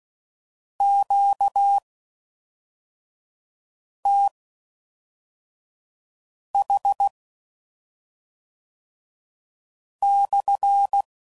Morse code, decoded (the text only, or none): QTH/